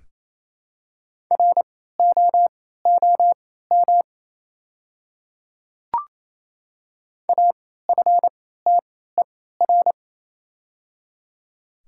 Morse code, heard as a keyed tone, about 28 wpm, spaced as for 16 wpm, 700 Hz. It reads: ROOM AFTER
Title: 3x Character Spacing - Top 500 Words - 28wpm